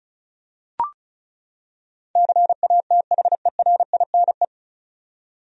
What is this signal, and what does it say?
Morse code, heard as a keyed tone, about 36 wpm, 700 Hz. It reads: CATHERINE
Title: USA Common Names - 35wpm